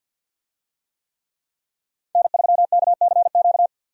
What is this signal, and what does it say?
Morse code, heard as a keyed tone, about 50 wpm, 700 Hz. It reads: N3KCQ
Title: USA Call Signs - 5 Characters - 50wpm